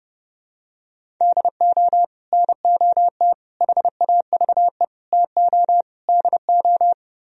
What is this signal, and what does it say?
Morse code, heard as a keyed tone, about 32 wpm, 700 Hz. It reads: DO NOT HAVE TO DO